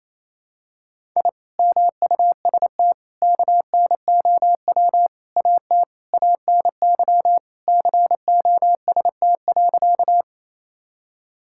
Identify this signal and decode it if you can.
Morse code, heard as a keyed tone, about 28 wpm, 700 Hz. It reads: I MUST KNOW AT ANY COST.
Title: Sentences from Top 1000 Words - Review Part 5 - 28wpm